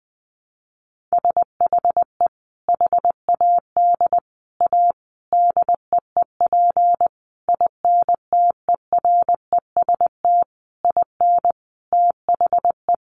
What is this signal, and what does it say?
Morse code, heard as a keyed tone, about 20 wpm, 700 Hz. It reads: SHE HAD A DEEP INTEREST IN THE
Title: Sentences from Top 1300 Words 20wpm